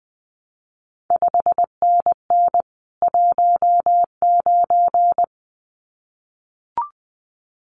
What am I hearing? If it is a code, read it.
Morse code, heard as a keyed tone, about 20 wpm, 700 Hz. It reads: 5NN 19
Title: CQ WW DX Contest Exchange Encore - 20wpm